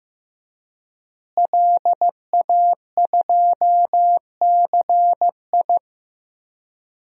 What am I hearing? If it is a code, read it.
Morse code, heard as a keyed tone, about 15 wpm, 700 Hz. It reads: LA2CI
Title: DXCC-100 Call Signs - 5 Characters - Repeated 2x - 15wpm